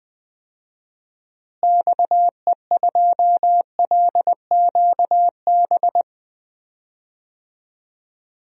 Morse code, heard as a keyed tone, about 20 wpm, 700 Hz. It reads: XE2LQB
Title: DXCC-100 Call Signs - 6 Characters - Repeated 3x - 20wpm